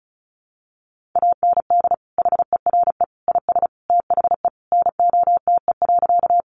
Morse code, heard as a keyed tone, about 34 wpm, 700 Hz. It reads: AND HERE IS THE NOTE.
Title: Sentences from Top 500 Words 35wpm